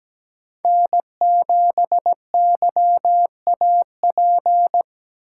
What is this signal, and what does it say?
Morse code, heard as a keyed tone, about 17 wpm, 700 Hz. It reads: N7YAP
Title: USA Call Signs - 5 Characters - 3x Character Spacing - 17wpm